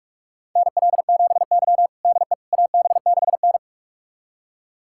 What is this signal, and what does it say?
Morse code, heard as a keyed tone, about 46 wpm, 700 Hz. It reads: NL7Y DE AB6N
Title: USA Callsign DE Callsign - 4 Characters - 45wpm